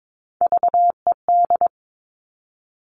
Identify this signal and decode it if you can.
Morse code, heard as a keyed tone, about 22 wpm, 700 Hz. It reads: VED
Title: All Cut Numbers - 3-Digit 22wpm